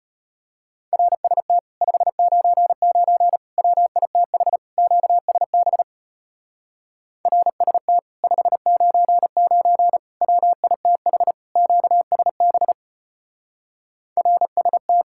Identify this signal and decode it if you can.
Morse code, heard as a keyed tone, about 32 wpm, 700 Hz. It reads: RST 599 WITH QSB RST 599 WITH QSB RST